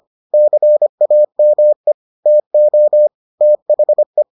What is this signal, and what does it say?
Morse code, heard as a keyed tone, about 25 wpm, 600 Hz. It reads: CAME TO THE